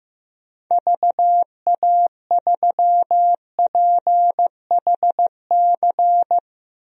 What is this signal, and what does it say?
Morse code, heard as a keyed tone, about 16 wpm, 700 Hz. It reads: VA3PHC
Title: DXCC-100 Call Signs - 6 Characters - Repeated 3x - 15wpm